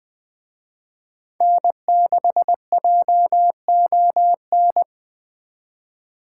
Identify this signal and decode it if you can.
Morse code, heard as a keyed tone, about 21 wpm, 700 Hz. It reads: N6JON
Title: Speed Racing - USA Call Signs 17wpm